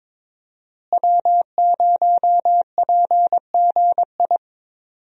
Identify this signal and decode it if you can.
Morse code, heard as a keyed tone, about 22 wpm, 700 Hz. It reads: W0PGI
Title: USA Call Signs - 5 Characters - Repeated 3x - 22wpm